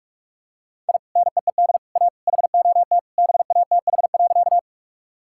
Morse code, heard as a keyed tone, about 44 wpm, 700 Hz. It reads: I NEED A HOT BATH.
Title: Sentences from Top 1600 Words 45wpm